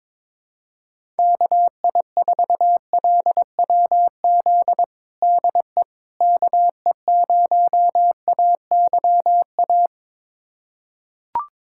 Morse code, heard as a keyed tone, about 22 wpm, 700 Hz. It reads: KI4LWZ DE KE0AYA
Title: USA Callsign DE Callsign - 6 Characters - 22wpm